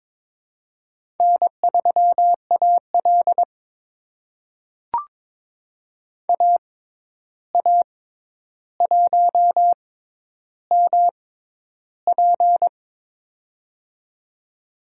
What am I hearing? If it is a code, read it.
Morse code, heard as a keyed tone, about 23 wpm, spaced as for 11 wpm, 700 Hz. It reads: N3AL AA1MP
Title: USA Call Signs - 6x Character Spacing - 22wpm